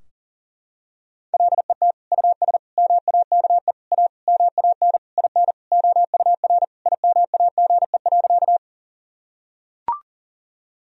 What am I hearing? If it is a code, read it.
Morse code, heard as a keyed tone, about 40 wpm, 700 Hz. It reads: LET US MAKE A MAN IN OUR IMAGE.
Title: Sentences from Top 1000 Words - Review Part 4 - 40wpm